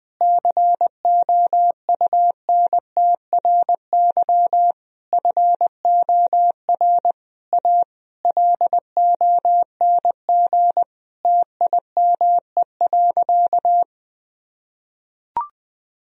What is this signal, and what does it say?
Morse code, heard as a keyed tone, about 20 wpm, 700 Hz. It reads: COUNTRY FOR A LONG TIME.
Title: Sentences from Top 1000 Words - Review Part 1 - 20wpm